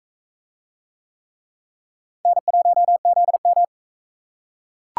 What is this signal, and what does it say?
Morse code, heard as a keyed tone, about 41 wpm, 700 Hz. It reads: N1ZM